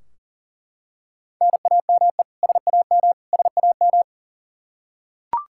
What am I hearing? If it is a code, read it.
Morse code, heard as a keyed tone, about 39 wpm, 700 Hz. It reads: NAME SAM SAM
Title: Practice QSO - Exchange 1 - Beginning - 40wpm